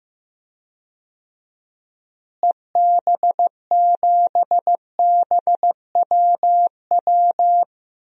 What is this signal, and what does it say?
Morse code, heard as a keyed tone, about 15 wpm, 700 Hz. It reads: EB7BWW
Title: DXCC-100 Call Signs - 6 Characters - Repeated 3x - 15wpm